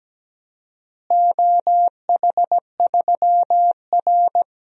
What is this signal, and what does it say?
Morse code, heard as a keyed tone, about 17 wpm, 700 Hz. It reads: OH3R
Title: DXCC-100 Call Signs - 4 Characters - Repeated 2x - 17wpm